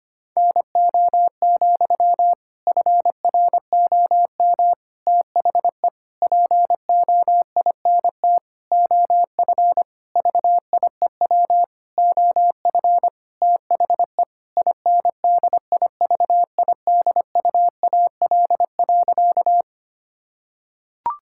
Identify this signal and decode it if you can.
Morse code, heard as a keyed tone, about 25 wpm, 700 Hz. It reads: NO, FROM THE POINT OF VIEW OF THE INDIVIDUAL.
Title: Sentences from Top 1000 Words - Review Part 6 - 25wpm